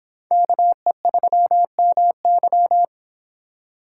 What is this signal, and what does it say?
Morse code, heard as a keyed tone, about 28 wpm, 700 Hz. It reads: KE3MY